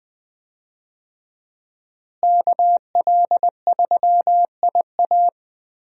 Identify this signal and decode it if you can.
Morse code, heard as a keyed tone, about 21 wpm, 700 Hz. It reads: KL3IA
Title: USA Call Signs - 5 Characters - Repeated 3x - 20wpm